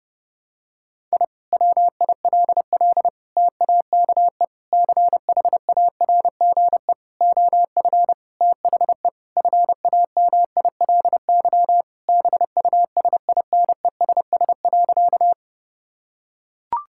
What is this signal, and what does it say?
Morse code, heard as a keyed tone, about 30 wpm, 700 Hz. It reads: I WILL TAKE CHARGE OF THE FAMILY BUSINESS.
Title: Sentences from Top 1000 Words - Review Part 6 - 30wpm